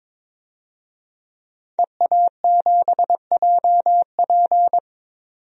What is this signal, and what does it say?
Morse code, heard as a keyed tone, about 22 wpm, 700 Hz. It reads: EA7JP
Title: DXCC-100 Call Signs - 5 Characters - 22wpm